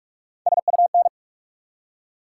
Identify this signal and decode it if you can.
Morse code, heard as a keyed tone, about 43 wpm, 700 Hz. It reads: SUN
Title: Three Letter Words - 45wpm